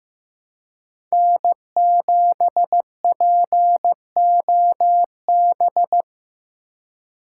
Morse code, heard as a keyed tone, about 16 wpm, 700 Hz. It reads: N7POB
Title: USA Call Signs - 5 Characters - 4x Character Spacing - 15wpm